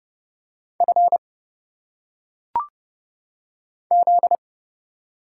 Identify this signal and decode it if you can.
Morse code, heard as a keyed tone, about 31 wpm, 700 Hz. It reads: F Z
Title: Single Letters - Phonetic Pronunciation 30wpm